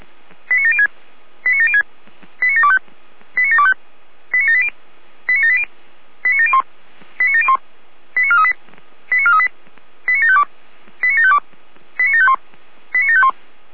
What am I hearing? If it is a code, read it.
Five-tone selective call (ZVEI), heard as quick runs of five tones, tones of about 80 ms. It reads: 89706 89706 89725 89725 89780 89780 89701 89701 89438 89438 89752 89752 89751 89751